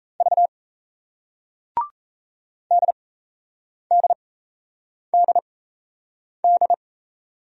Morse code, heard as a keyed tone, about 37 wpm, spaced as for 8 wpm, 700 Hz. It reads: V DDDD